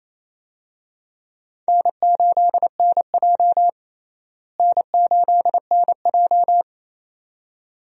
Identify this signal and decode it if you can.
Morse code, heard as a keyed tone, about 28 wpm, 700 Hz. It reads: N8NJ N8NJ
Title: USA Call Signs - 4 Characters - Repeated 2x - 28wpm